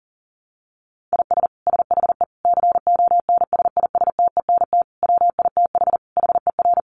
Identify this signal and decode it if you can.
Morse code, heard as a keyed tone, about 40 wpm, 700 Hz. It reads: IS SHE CONSISTENT WITH HER
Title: Sentences from Top 1500 Words 40wpm